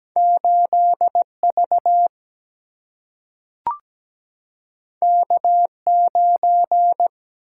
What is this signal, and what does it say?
Morse code, heard as a keyed tone, about 17 wpm, 700 Hz. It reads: 8V K9
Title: Single-Letter Single-Number 17wpm